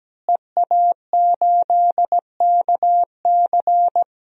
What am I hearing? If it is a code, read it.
Morse code, heard as a keyed tone, about 17 wpm, 700 Hz. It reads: EA8KC
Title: DXCC-100 Call Signs - 5 Characters - Repeated 2x - 17wpm